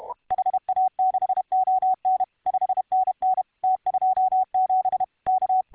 Morse code, heard as a keyed tone, about 31 wpm, 750 Hz. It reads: HA6ON 5NN T27 K